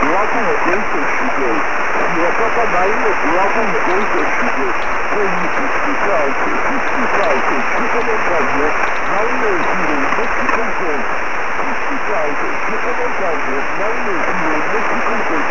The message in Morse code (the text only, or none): none